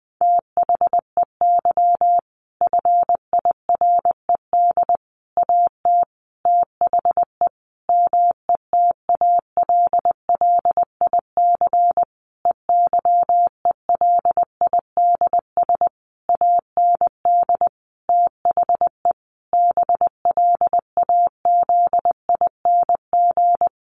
Morse code, heard as a keyed tone, about 20 wpm, 700 Hz. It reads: THEY FIRED AT THE METALLIC EYELIDS AND THE BLAZING
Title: A Sound Of Thunder 20wpm